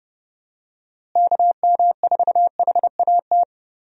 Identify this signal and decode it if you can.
Morse code, heard as a keyed tone, about 30 wpm, 700 Hz. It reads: KM4HAT
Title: USA Call Signs - 6 Characters - Repeated 2x - 30wpm